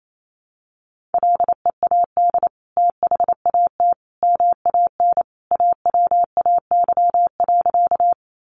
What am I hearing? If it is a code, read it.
Morse code, heard as a keyed tone, about 28 wpm, 700 Hz. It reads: LEAD THAT MAN AWAY.
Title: Sentences from Top 600 Words 28wpm